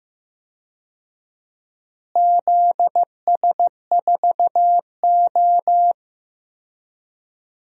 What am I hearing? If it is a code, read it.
Morse code, heard as a keyed tone, about 15 wpm, 700 Hz. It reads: ZS4O
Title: DXCC-100 Call Signs - 4 Characters - Repeated 3x - 15wpm